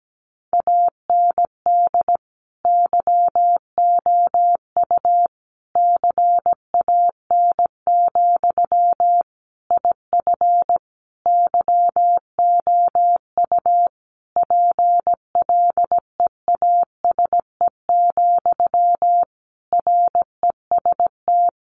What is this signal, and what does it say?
Morse code, heard as a keyed tone, about 17 wpm, 700 Hz. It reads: AND YOU CAN, IF YOU PLEASE, REST
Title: Sentences from Top 1500 Words 17wpm